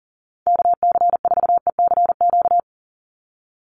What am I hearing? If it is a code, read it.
Morse code, heard as a keyed tone, about 38 wpm, 700 Hz. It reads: KC4ECQ
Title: USA Call Signs 40wpm